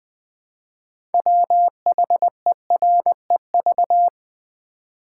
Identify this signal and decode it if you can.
Morse code, heard as a keyed tone, about 20 wpm, 700 Hz. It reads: WHEREV
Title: Build A Word - 2 Letters At A Time - 8-Letter Words - 20wpm